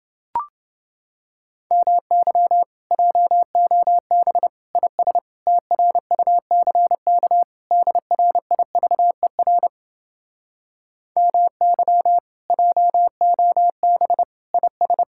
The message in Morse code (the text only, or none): MY JOB IS TRUCK DRIVER MY JOB IS